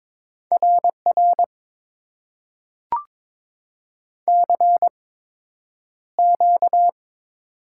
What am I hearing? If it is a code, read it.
Morse code, heard as a keyed tone, about 22 wpm, spaced as for 9 wpm, 700 Hz. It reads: RR CQ
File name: Top 90 QSO Elements - 8x Character Spacing - 22wpm